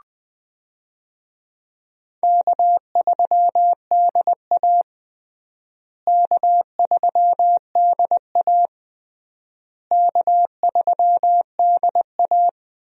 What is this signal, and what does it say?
Morse code, heard as a keyed tone, about 20 wpm, 700 Hz. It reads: K3DA K3DA K3DA
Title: USA Call Signs - 4 Characters - Repeated 3x - 20wpm